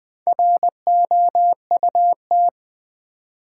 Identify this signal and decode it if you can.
Morse code, heard as a keyed tone, about 20 wpm, 700 Hz. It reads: ROUT
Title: Build A Word - 1 Letter At A Time - 3 to 5-Letter Words - 20wpm